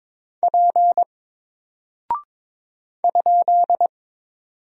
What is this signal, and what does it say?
Morse code, heard as a keyed tone, about 22 wpm, 700 Hz. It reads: P ?
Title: Single Character (Letter:Number:Punctuation:Prosign) Alt 22wpm